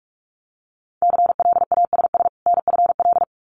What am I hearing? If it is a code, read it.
Morse code, heard as a keyed tone, about 46 wpm, 700 Hz. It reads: CLASS NFL
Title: Winter Field Day Exchanges 45wpm